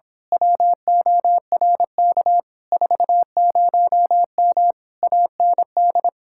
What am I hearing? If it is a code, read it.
Morse code, heard as a keyed tone, about 26 wpm, 700 Hz. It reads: WORK 40M AND
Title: Speed-Racing Common QSO Phrases - 22wpm